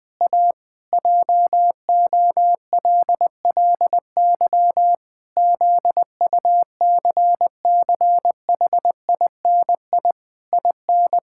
Morse code, heard as a keyed tone, about 20 wpm, 700 Hz. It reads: A JOLLY ZUCCHINI IN
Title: Pangram Challenge v2 - 20wpm